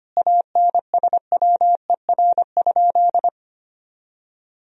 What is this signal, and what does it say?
Morse code, heard as a keyed tone, about 25 wpm, 700 Hz. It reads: ANSWER?
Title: 3x Word-Spacing - Sentences from Top 300 Words - 25wpm